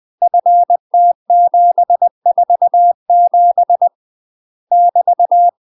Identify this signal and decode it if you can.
Morse code, heard as a keyed tone, about 20 wpm, 700 Hz. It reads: FT747 <BT>
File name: UK MASTER QSOs TRAINING / 20WPM / UK MASTER QSOs 0002